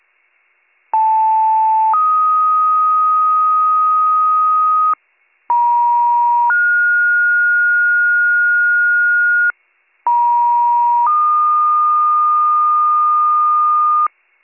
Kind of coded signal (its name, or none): two-tone page